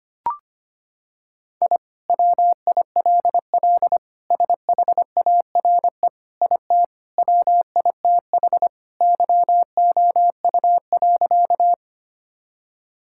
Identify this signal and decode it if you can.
Morse code, heard as a keyed tone, about 25 wpm, 700 Hz. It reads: I WILL SHARE IT WITH YOU.
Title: Sentences from Top 1000 Words - Review Part 6 - 25wpm